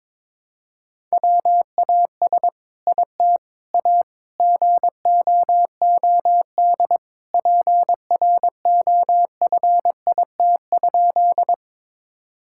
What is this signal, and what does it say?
Morse code, heard as a keyed tone, about 22 wpm, 700 Hz. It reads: WAS IT A GOOD PROFIT?
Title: Sentences from Top 1000 Words - Review Part 5 - 22wpm